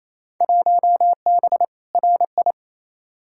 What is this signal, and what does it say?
Morse code, heard as a keyed tone, about 28 wpm, 700 Hz. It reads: 1B RI
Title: Field Day Exchanges Encore - 28wpm